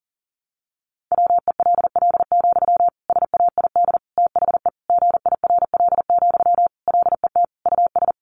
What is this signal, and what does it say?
Morse code, heard as a keyed tone, about 40 wpm, 700 Hz. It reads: WELL, SAID THE GIRL, LET US
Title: Sentences from Top 400 Words 40wpm